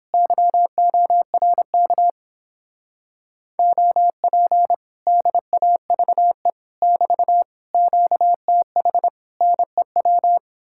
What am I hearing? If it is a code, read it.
Morse code, heard as a keyed tone, about 31 wpm, 700 Hz. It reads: YORK OP DAVE <BT> QTH NEW